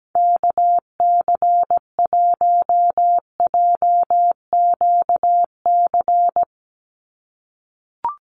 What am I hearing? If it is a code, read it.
Morse code, heard as a keyed tone, about 17 wpm, 700 Hz. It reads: KC1JQC
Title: USA Call Signs 17wpm